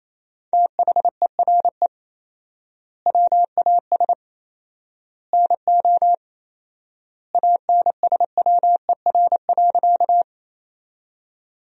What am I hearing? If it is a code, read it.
Morse code, heard as a keyed tone, about 27 wpm, 700 Hz. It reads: THERE WAS NO ANSWER.
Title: 4x Word-Spacing - Sentences from Top 300 Words - 28wpm